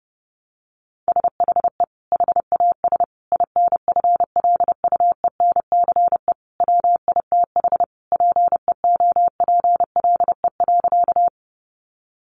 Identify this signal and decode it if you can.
Morse code, heard as a keyed tone, about 30 wpm, 700 Hz. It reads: SHE HAS INFLUENCE WITH PEOPLE.
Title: Sentences from Top 1100 Words 30wpm